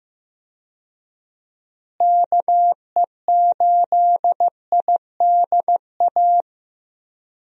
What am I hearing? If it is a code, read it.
Morse code, heard as a keyed tone, about 15 wpm, 700 Hz. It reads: KE8IDA
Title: USA Call Signs - 6 Characters - Repeated 3x - 15wpm